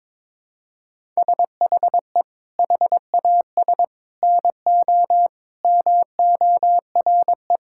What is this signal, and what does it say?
Morse code, heard as a keyed tone, about 23 wpm, 700 Hz. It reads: SHE HAS NO MORE
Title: Sentences from Top 1000 Words - Review Part 6 - 22wpm